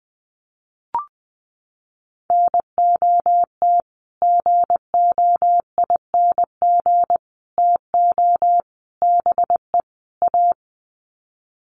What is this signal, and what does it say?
Morse code, heard as a keyed tone, about 20 wpm, 700 Hz. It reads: NOT GOING TO BE A
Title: Sets of 5 Words Encore 20wpm